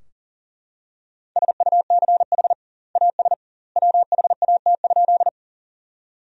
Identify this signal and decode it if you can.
Morse code, heard as a keyed tone, about 40 wpm, 700 Hz. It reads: SUCH AS WHAT?
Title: 2x Word-Spacing - Sentences from Top 200 Words - 40wpm